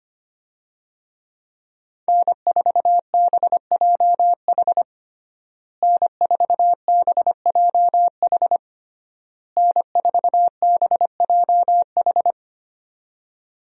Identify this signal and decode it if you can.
Morse code, heard as a keyed tone, about 25 wpm, 700 Hz. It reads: N4BJH N4BJH N4BJH
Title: USA Call Signs - 5 Characters - Repeated 3x - 25wpm